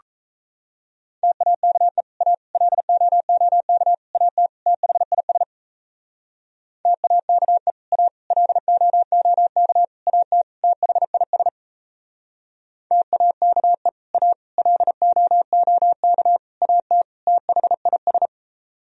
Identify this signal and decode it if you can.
Morse code, heard as a keyed tone, about 38 wpm, 700 Hz. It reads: TAKE A LOOK AT THIS TAKE A LOOK AT THIS TAKE A LOOK AT THIS